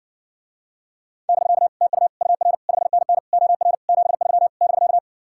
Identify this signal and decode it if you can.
Morse code, heard as a keyed tone, about 60 wpm, 700 Hz. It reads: <BK> TU UR 5NN OR 73 <BK>